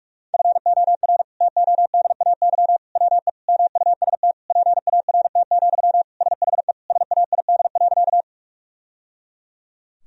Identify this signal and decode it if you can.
Morse code, heard as a keyed tone, about 46 wpm, 700 Hz. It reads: FOR TODAY WE MUST PART, SHE SAID.
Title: Sentences from Top 1000 Words - Review Part 1 - 45wpm